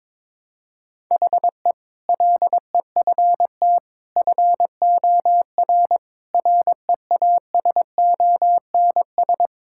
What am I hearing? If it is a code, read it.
Morse code, heard as a keyed tone, about 22 wpm, 700 Hz. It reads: HE LEFT FOR REASONS
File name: practice / 22wpm / Sentences with 8-Word Phrases - 22wpm